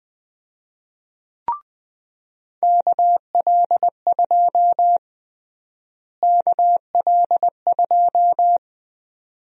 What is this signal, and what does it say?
Morse code, heard as a keyed tone, about 20 wpm, 700 Hz. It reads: KL2 KL2
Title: USA Call Sign Prefixes - Repeated 2x - 20wpm